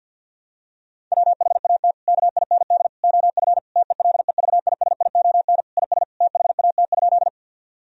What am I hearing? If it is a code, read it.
Morse code, heard as a keyed tone, about 50 wpm, 700 Hz. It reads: WHAT KIND OF TELEVISION IS THAT?
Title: Sentences from Top 1000 Words - Review Part 4 - 50wpm